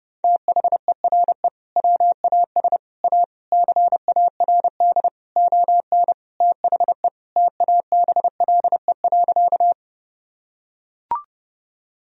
Morse code, heard as a keyed tone, about 30 wpm, 700 Hz. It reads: THERE WAS A CARD ON THE TABLE.
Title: Sentences from Top 1000 Words - Review Part 4 - 30wpm